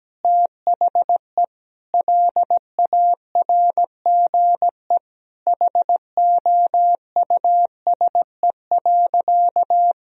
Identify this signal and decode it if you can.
Morse code, heard as a keyed tone, about 17 wpm, 700 Hz. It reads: THE LARGE HOUSE.
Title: Sentences from Top 1000 Words - Review Part 1 - 17wpm